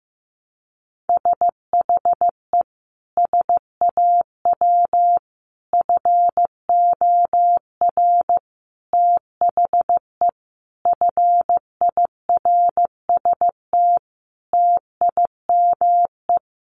Morse code, heard as a keyed tone, about 15 wpm, 700 Hz. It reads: SHE SAW FOR THE FIRST TIME
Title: Sentences from Top 1200 Words 15wpm